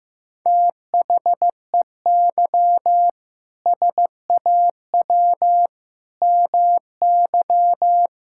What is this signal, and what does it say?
Morse code, heard as a keyed tone, about 16 wpm, 700 Hz. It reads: THEY SAW MY
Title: Sentences from Top 1600 Words 15wpm